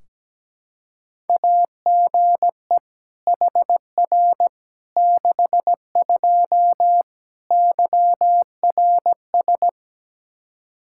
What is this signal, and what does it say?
Morse code, heard as a keyed tone, about 17 wpm, 700 Hz. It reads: AGE HR 62 YRS
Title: Practice QSO - Exchange 3 - More Info - 17wpm